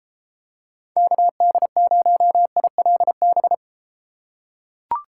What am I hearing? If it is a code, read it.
Morse code, heard as a keyed tone, about 32 wpm, 700 Hz. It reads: KD0ILB